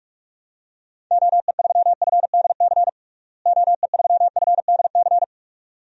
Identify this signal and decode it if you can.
Morse code, heard as a keyed tone, about 46 wpm, 700 Hz. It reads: OE3FDC OE3FDC
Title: DXCC-100 Call Signs - 6 Characters - Repeated 2x - 45wpm